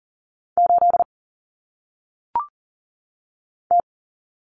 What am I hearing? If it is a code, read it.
Morse code, heard as a keyed tone, about 39 wpm, 700 Hz. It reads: 8 T
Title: Common Cut Numbers - 1-Digit 40wpm